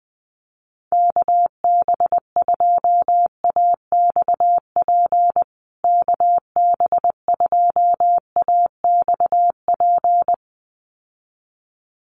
Morse code, heard as a keyed tone, about 20 wpm, 700 Hz. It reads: KB2AXP KB2AXP
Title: USA Call Signs Repeated 2x 20wpm